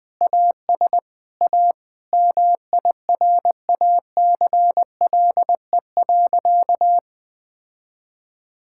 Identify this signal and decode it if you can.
Morse code, heard as a keyed tone, about 20 wpm, 700 Hz. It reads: AS A MIRACLE.
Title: Sentences with 7-Word Phrases - Spoken Beginning - 20wpm